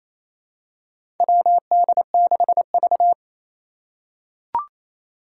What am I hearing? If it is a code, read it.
Morse code, heard as a keyed tone, about 27 wpm, 700 Hz. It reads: WD6V
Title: Four Challenging Letter-Number Combinations 28wpm